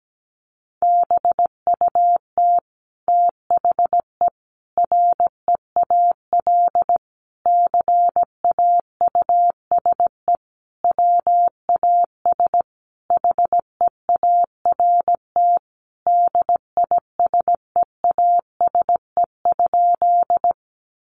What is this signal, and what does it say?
Morse code, heard as a keyed tone, about 17 wpm, 700 Hz. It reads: BUT THE REAL CAUSE WAS HEART DISEASE?
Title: Sentences from Top 1200 Words 17wpm